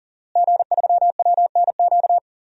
Morse code, heard as a keyed tone, about 38 wpm, 700 Hz. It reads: G3WNQ
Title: DXCC-100 Call Signs - 5 Characters - Repeated 3x - 40wpm